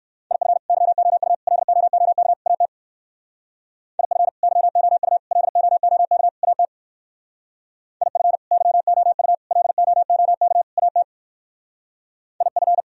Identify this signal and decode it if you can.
Morse code, heard as a keyed tone, about 58 wpm, 700 Hz. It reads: IF YOU LOOK AT IF YOU LOOK AT IF YOU LOOK AT IF